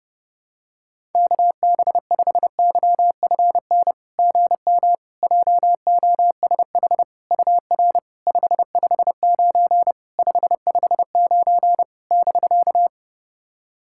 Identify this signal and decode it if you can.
Morse code, heard as a keyed tone, about 30 wpm, 660 Hz. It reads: KB5YFN GM JOSH UR 559 559 <BK>
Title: SOTA QSO Practice 30wpm